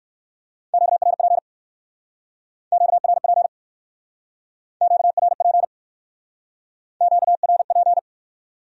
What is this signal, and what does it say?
Morse code, heard as a keyed tone, about 54 wpm, 700 Hz. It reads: QRP QRP QRP QRP
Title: Speed-Racing Q-Codes - 45wpm